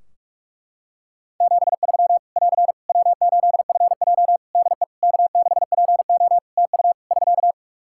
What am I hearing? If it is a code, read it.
Morse code, heard as a keyed tone, about 46 wpm, 700 Hz. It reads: 73 <AR> W8FJ DE K6PO TU <SK>
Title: Practice QSO - Exchange 4 - Ending - 45wpm